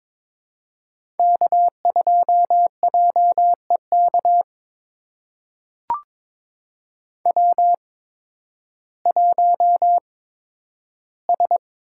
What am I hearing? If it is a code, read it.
Morse code, heard as a keyed tone, about 22 wpm, spaced as for 12 wpm, 700 Hz. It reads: K2JEK W1S
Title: USA Call Signs - 5 Characters - 8x Character Spacing - 22wpm